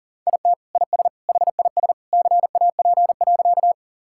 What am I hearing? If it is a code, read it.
Morse code, heard as a keyed tone, about 40 wpm, 700 Hz. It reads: IT IS HIS CAP.
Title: Sentences from Top 1600 Words 40wpm